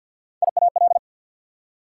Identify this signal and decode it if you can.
Morse code, heard as a keyed tone, about 53 wpm, 690 Hz. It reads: IAL
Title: Prefixes and Suffixes 50wpm